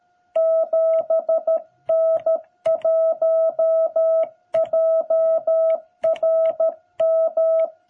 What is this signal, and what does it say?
Morse code, heard as a keyed tone, about 13 wpm, 660 Hz. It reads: 7N1JRM
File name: jh8cbh / 2021cw